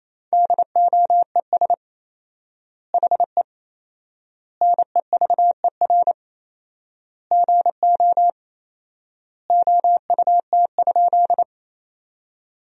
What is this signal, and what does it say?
Morse code, heard as a keyed tone, about 27 wpm, 700 Hz. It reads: DOES HE NEVER GO OUT?
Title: 4x Word-Spacing - Sentences from Top 300 Words - 28wpm